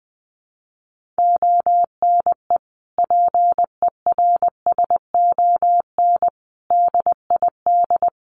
Morse code, heard as a keyed tone, about 20 wpm, 700 Hz. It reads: ONE PERSON DID
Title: Sentences from Top 1500 Words 20wpm